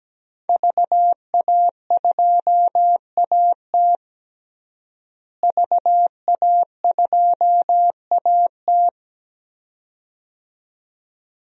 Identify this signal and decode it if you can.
Morse code, heard as a keyed tone, about 17 wpm, 700 Hz. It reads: VA2AT VA2AT
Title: DXCC-100 Call Signs - 5 Characters - Repeated 2x - 17wpm